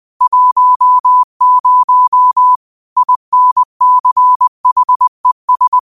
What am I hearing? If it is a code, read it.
Morse code, heard as a keyed tone, about 20 wpm, 1000 Hz. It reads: 10 INCHES
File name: qotdfiles / 20wpm